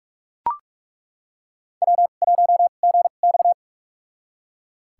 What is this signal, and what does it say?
Morse code, heard as a keyed tone, about 41 wpm, spaced as for 31 wpm, 700 Hz. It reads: W1GX
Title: USA Call Signs - 4 Characters - 2x Character Spacing - 45wpm